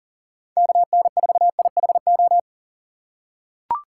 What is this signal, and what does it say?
Morse code, heard as a keyed tone, about 40 wpm, 700 Hz. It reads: KN4IHO
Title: USA Call Signs - 6 Characters - 4x Character Spacing - 40wpm